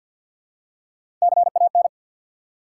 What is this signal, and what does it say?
Morse code, heard as a keyed tone, about 51 wpm, 700 Hz. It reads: CAN
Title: Top 180 QSO Elements 50wpm